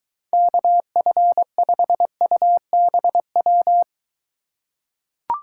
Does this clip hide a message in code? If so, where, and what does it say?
Yes: Morse code from 0.3 to 3.8 s, heard as a keyed tone, about 23 wpm, 700 Hz. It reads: KF5UBW